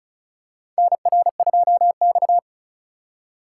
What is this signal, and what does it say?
Morse code, heard as a keyed tone, about 36 wpm, 690 Hz. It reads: NR2X
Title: USA Call Signs - 4 Characters - 2x Character Spacing - 35wpm